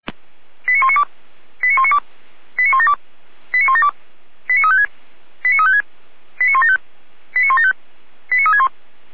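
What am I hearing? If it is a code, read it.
Five-tone selective call (ZVEI), heard as quick runs of five tones, tones of about 70 ms. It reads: 89192 89192 89172 89172 89356 89356 89175 89175 89361
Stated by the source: ZVEI five-tone selective call 89018, received by SDR.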